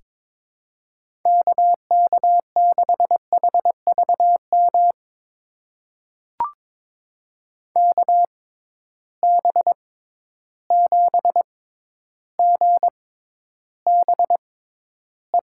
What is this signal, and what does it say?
Morse code, heard as a keyed tone, about 22 wpm, spaced as for 11 wpm, 700 Hz. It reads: KK6HVM KB7GBE
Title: USA Call Signs - 6 Characters - 6x Character Spacing - 22wpm